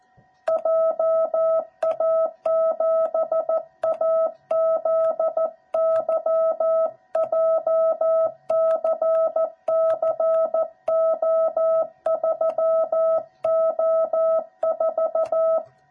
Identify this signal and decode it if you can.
Morse code, heard as a keyed tone, about 14 wpm, 660 Hz. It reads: JA7AZY JCCO3O4
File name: jh8cbh / 2021cw / cw06b